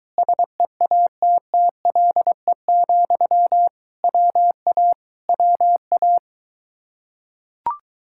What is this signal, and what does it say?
Morse code, heard as a keyed tone, about 23 wpm, 700 Hz. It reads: SEATTLE, WA WA